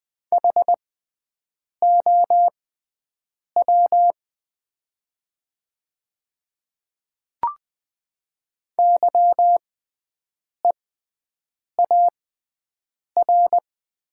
Morse code, heard as a keyed tone, about 20 wpm, spaced as for 7 wpm, 700 Hz. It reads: HOW YEAR
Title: 6x Character Spacing - Top 200 Words - 20wpm